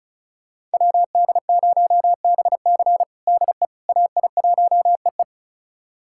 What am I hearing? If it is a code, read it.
Morse code, heard as a keyed tone, about 34 wpm, 700 Hz. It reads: WD0BC DE AI1EE
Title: USA Callsign DE Callsign - 5 Characters - 35wpm